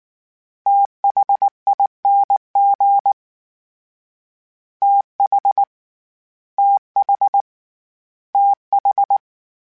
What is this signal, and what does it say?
Morse code, heard as a keyed tone, about 19 wpm, 800 Hz. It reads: THING TH TH TH
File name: THs / TH / TH Prefixes 3 @18